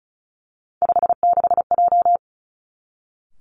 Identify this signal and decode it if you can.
Morse code, heard as a keyed tone, about 36 wpm, 700 Hz. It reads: 56J
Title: Three Challenging Letter-Number Combinations 35wpm